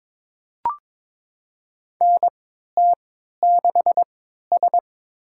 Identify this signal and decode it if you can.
Morse code, heard as a keyed tone, about 22 wpm, spaced as for 12 wpm, 700 Hz. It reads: NT6S
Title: USA Call Signs - 4 Characters - 3x Character Spacing - 22wpm